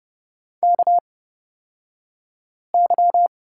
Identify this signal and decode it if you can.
Morse code, heard as a keyed tone, about 30 wpm, spaced as for 5 wpm, 700 Hz. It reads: KY
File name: Lesson 38d - Letter q - Two Characters 30wpm